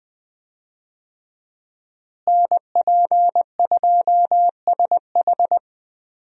Morse code, heard as a keyed tone, about 20 wpm, 700 Hz. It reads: NP2SH
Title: USA Call Signs - 5 Characters - Repeated 2x - 20wpm